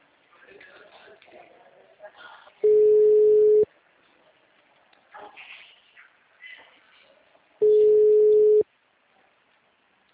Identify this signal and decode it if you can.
Ringback tone (Europe, 425 Hz), with speech.
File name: out- -2011- - - .93966